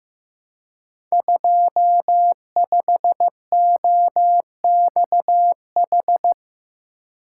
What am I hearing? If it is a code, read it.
Morse code, heard as a keyed tone, about 15 wpm, 700 Hz. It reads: 25OXH
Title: Code Groups - 5 Letter:Numbers - 15wpm